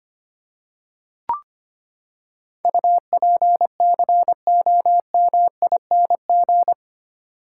Morse code, heard as a keyed tone, about 25 wpm, 700 Hz. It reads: UPCOMING
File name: Eight Letter Words - 25wpm